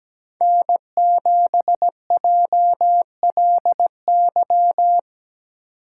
Morse code, heard as a keyed tone, about 17 wpm, 700 Hz. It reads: N7JLY
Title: USA Call Signs - 5 Characters - 2x Character Spacing - 17wpm